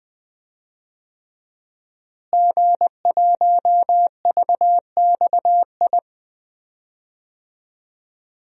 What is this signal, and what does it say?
Morse code, heard as a keyed tone, about 20 wpm, 700 Hz. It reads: G1VXI
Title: DXCC-100 Call Signs - 5 Characters - Repeated 3x - 20wpm